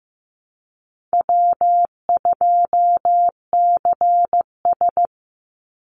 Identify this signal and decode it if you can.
Morse code, heard as a keyed tone, about 15 wpm, 700 Hz. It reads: W2CS
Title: Call Signs Repeated 3x - 15wpm